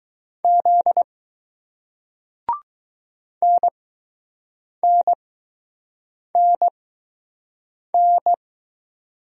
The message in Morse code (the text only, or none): Z NNNN